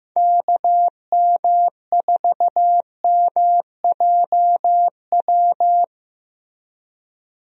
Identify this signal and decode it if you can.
Morse code, heard as a keyed tone, about 15 wpm, 700 Hz. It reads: KM4MJW
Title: USA Call Signs - 6 Characters - Repeated 3x - 15wpm